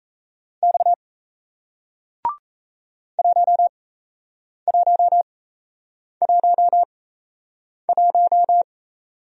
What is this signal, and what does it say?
Morse code, heard as a keyed tone, about 37 wpm, spaced as for 8 wpm, 700 Hz. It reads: X 1111